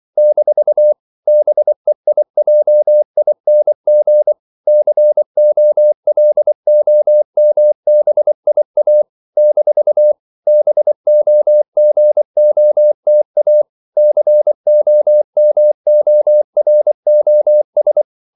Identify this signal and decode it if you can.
Morse code, heard as a keyed tone, about 24 wpm, 600 Hz. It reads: - BEIJING COLOMBIA - BOGOTA COMOROS
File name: CW / Practice / NationalCapitals24